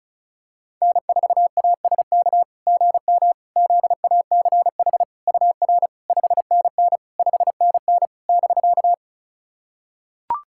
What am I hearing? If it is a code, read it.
Morse code, heard as a keyed tone, about 36 wpm, 690 Hz. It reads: N4ASK GM ZACH UR 5NN 5NN <BK>
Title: SOTA QSO Practice 35wpm